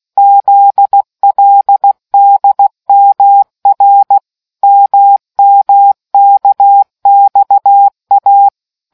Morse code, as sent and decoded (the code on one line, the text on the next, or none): --.. .-.. -.. -- .-. / -- -- -.- -..- .-
ZLDMR MMKXA